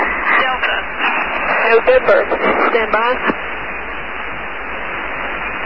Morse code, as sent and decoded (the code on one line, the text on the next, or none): none